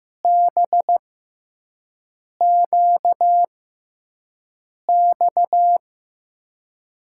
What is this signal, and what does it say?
Morse code, heard as a keyed tone, about 15 wpm, spaced as for 5 wpm, 700 Hz. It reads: BQX
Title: USA Call Signs - 6 Characters - 6x Character Spacing - 15wpm